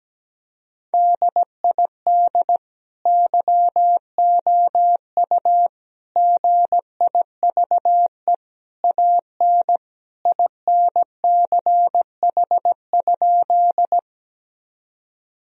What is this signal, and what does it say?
Morse code, heard as a keyed tone, about 17 wpm, 700 Hz. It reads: DID YOU GIVE AN INCH?
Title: Sentences from Top 1000 Words - Review Part 3 - 17wpm